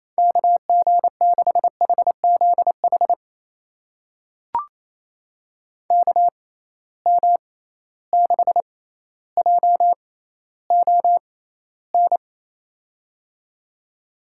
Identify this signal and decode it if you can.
Morse code, heard as a keyed tone, about 28 wpm, spaced as for 14 wpm, 700 Hz. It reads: KG6HZH KM6JON
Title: USA Call Signs - 6 Characters - 6x Character Spacing - 28wpm